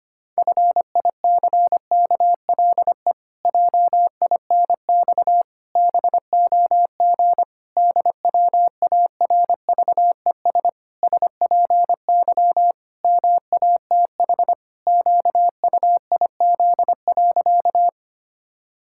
Morse code, heard as a keyed tone, about 25 wpm, 700 Hz. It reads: FICKLE JINX BOG DWARVES SPY MATH QUIZ.
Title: Pangram Challenge v2 - 25wpm